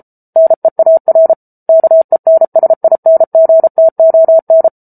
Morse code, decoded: NEAR KENSINGTON

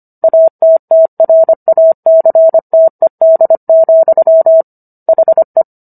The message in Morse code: ATTRACTED, HE

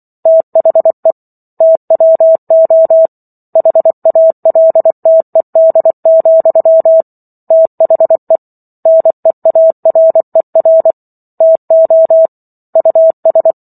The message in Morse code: THE TWO HALTED, THE NEARER TO US

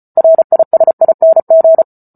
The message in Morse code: RISING

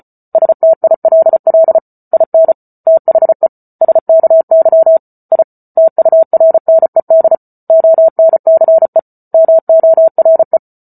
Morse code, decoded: STILL IN THE SKY I TURNED ONCE MORE

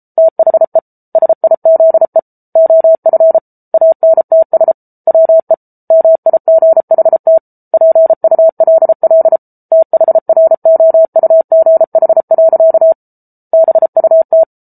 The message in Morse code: THE SIZE OF ANTS WE MIGHT PULL THROUGH. BUT

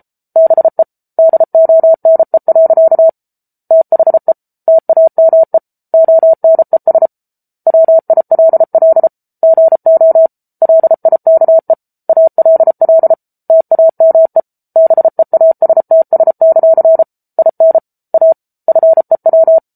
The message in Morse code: BE DONE. THE TAME ONES WILL GO LIKE ALL TAME BEASTS; IN A FEW

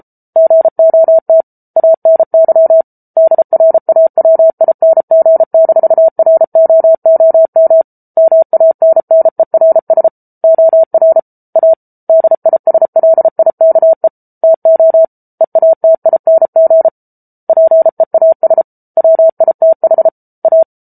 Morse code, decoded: GOT ANY DRAWING-ROOM MANNERS OR A DISLIKE TO EATING PEAS WITH A

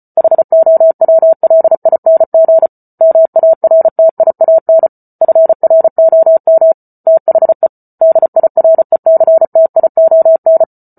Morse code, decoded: HOWLING MARTIAN FROM THE DIRECTION

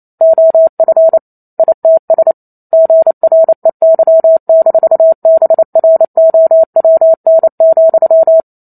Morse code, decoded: OF ITS GREY-BROWN,

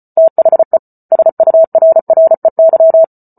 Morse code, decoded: THE SURREY

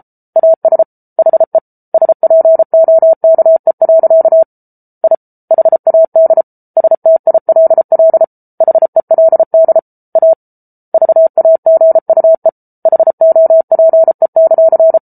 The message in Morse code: AS HE SPOKE. I HAD STILL HELD A VAGUE HOPE;

